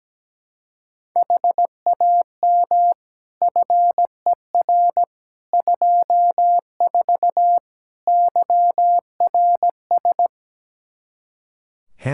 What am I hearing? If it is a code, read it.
Morse code, heard as a keyed tone, about 17 wpm, 700 Hz. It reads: HAM FER 24 YRS
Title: Practice QSO - Exchange 3 - More Info - 17wpm